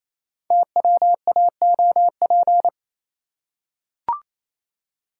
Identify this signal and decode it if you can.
Morse code, heard as a keyed tone, about 27 wpm, 700 Hz. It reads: TWAOP
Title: Code Groups - 5 Letters - 28wpm